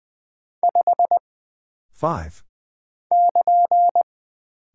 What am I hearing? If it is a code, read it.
Morse code, heard as a keyed tone, about 20 wpm, 700 Hz. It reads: 5 <KN>
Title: Single Character (Letter:Number:Punctuation:Prosign) - Mind-Melt - 20wpm